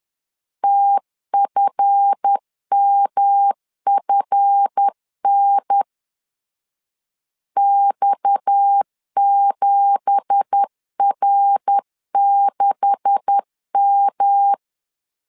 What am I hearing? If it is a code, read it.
Morse code, heard as a keyed tone, about 10 wpm, 800 Hz. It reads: TFMFN X7R6M
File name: CW / DET / 1 9 1500C S A 10wpm 4511